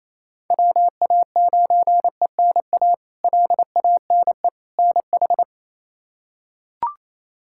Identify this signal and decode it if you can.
Morse code, heard as a keyed tone, about 28 wpm, 700 Hz. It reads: WA9ENA LANE NH
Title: K1USN SST QSO Practice - 28wpm